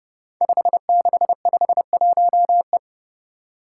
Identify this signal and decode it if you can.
Morse code, heard as a keyed tone, about 30 wpm, 700 Hz. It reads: 5651E
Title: Five Challenging Letter-Number Combinations 30wpm